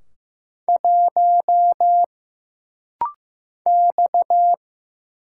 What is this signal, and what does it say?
Morse code, heard as a keyed tone, about 15 wpm, 700 Hz. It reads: 1 X
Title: Single Letter-Number Alt - Rapid-Fire 15wpm